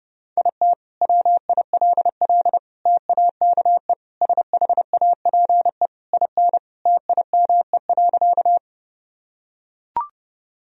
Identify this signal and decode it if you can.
Morse code, heard as a keyed tone, about 30 wpm, 700 Hz. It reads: IT WILL TAKE SHAPE IN TIME.
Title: Sentences from Top 1000 Words - Review Part 3 - 30wpm